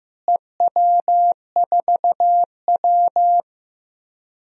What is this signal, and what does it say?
Morse code, heard as a keyed tone, about 15 wpm, 700 Hz. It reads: EW4W
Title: DXCC-100 Call Signs - 4 Characters - Repeated 3x - 15wpm